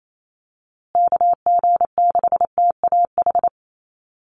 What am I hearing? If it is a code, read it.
Morse code, heard as a keyed tone, about 28 wpm, 700 Hz. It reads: KG6TAH
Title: USA Call Signs Repeated 3x 28wpm